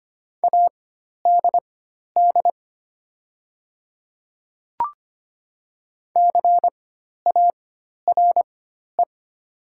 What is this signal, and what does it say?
Morse code, heard as a keyed tone, about 25 wpm, spaced as for 12 wpm, 700 Hz. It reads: ADD CARE
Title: 4x Character Spacing - Top 300 Words - 25wpm